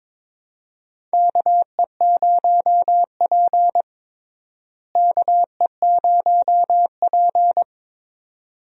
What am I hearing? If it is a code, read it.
Morse code, heard as a keyed tone, about 21 wpm, 700 Hz. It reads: KE0P KE0P
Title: USA Call Signs - 4 Characters - Repeated 2x - 22wpm